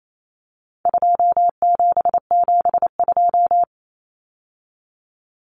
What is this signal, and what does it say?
Morse code, heard as a keyed tone, about 28 wpm, 700 Hz. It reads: 2772
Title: Common Cut Numbers - 4-Digit 28wpm